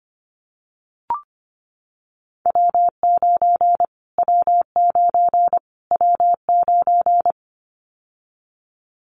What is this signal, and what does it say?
Morse code, heard as a keyed tone, about 25 wpm, 700 Hz. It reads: W9 W9 W9
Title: Callsign Prefixes for Top 25 DXCC - Repeated 3x - 25wpm